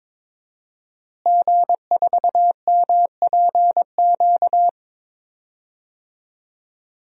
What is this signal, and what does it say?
Morse code, heard as a keyed tone, about 22 wpm, 700 Hz. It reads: G4MPQ
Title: DXCC-100 Call Signs - 5 Characters - Repeated 3x - 22wpm